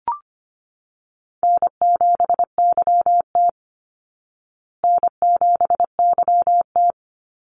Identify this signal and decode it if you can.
Morse code, heard as a keyed tone, about 25 wpm, 700 Hz. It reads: N7YT N7YT
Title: Easy Call Signs Repeated 3x - 25wpm